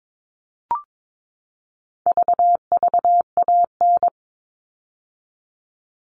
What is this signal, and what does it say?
Morse code, heard as a keyed tone, about 22 wpm, 700 Hz. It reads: VVAN
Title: All Cut Numbers - 4-Digit 22wpm